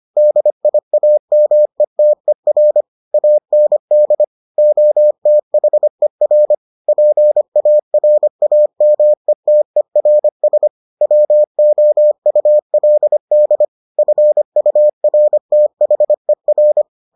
Morse code, heard as a keyed tone, about 25 wpm, 600 Hz. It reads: DIAMETER AND OTHER PARAMETERS WOULD FURTHER